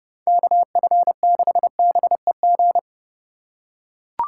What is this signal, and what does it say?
Morse code, heard as a keyed tone, about 29 wpm, 700 Hz. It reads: KF6BEG